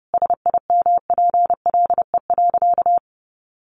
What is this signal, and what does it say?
Morse code, heard as a keyed tone, about 30 wpm, 700 Hz. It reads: SIMPLE.
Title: Sentences from Top 500 Words 30wpm